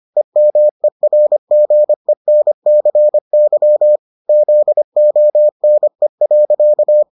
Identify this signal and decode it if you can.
Morse code, heard as a keyed tone, about 25 wpm, 600 Hz. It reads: EMERGENCY ZONE.